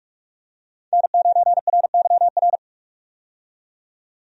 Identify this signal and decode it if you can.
Morse code, heard as a keyed tone, about 46 wpm, 700 Hz. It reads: N9RYR
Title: USA Call Signs - 5 Characters - 45wpm